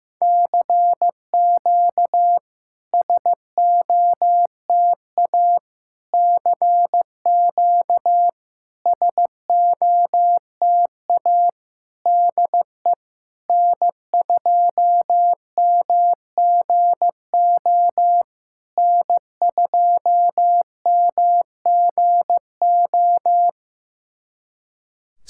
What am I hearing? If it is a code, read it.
Morse code, heard as a keyed tone, about 15 wpm, 700 Hz. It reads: CQ SOTA CQ SOTA DE N2MGO N2MGO
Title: SOTA QSO Practice 15wpm